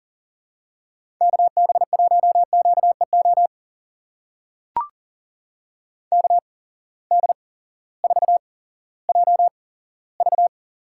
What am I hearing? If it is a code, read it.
Morse code, heard as a keyed tone, about 39 wpm, spaced as for 18 wpm, 700 Hz. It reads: KB1QEO KD4JV
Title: USA Call Signs - 6 Characters - 8x Character Spacing - 40wpm